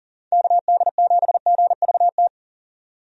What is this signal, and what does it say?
Morse code, heard as a keyed tone, about 40 wpm, 700 Hz. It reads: KD7GVT
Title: USA Call Signs - 6 Characters - 40wpm